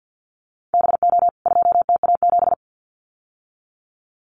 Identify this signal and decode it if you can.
Morse code, heard as a keyed tone, about 50 wpm, 700 Hz. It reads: BO 2TA7
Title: CWT Exchange 50wpm